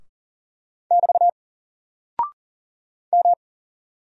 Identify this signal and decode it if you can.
Morse code, heard as a keyed tone, about 40 wpm, 700 Hz. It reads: <BT> M
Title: Single Character (Letter:Number:Punctuation:Prosign) Alt 40wpm